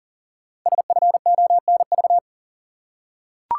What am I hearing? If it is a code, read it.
Morse code, heard as a keyed tone, about 41 wpm, 700 Hz. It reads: SFONV
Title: Code Groups - 5 Letters - 40wpm